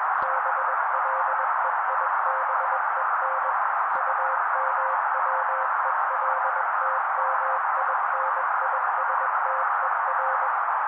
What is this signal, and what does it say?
Morse code, heard as a keyed tone, about 21 wpm, 530 Hz. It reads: BLEIBEN UMWELTMINISTER